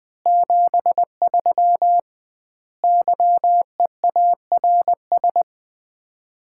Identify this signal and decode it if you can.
Morse code, heard as a keyed tone, about 20 wpm, 700 Hz. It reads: 73 YEARS
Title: Common QSO Phrases - 2x Word-Spacing - 20wpm